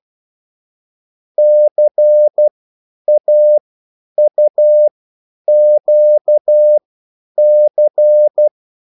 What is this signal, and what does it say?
Morse code, heard as a keyed tone, about 12 wpm, spaced as for 9 wpm, 600 Hz. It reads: CAUQC
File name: QSL / CW lesson